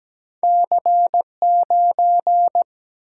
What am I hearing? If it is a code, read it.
Morse code, heard as a keyed tone, about 17 wpm, 700 Hz. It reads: C9
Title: Single-Letter Single-Number 17wpm